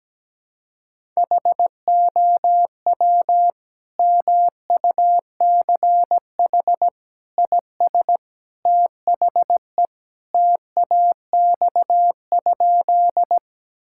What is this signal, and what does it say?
Morse code, heard as a keyed tone, about 17 wpm, 700 Hz. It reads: HOW MUCH IS THE TAX?
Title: Sentences from Top 1000 Words - Review Part 5 - 17wpm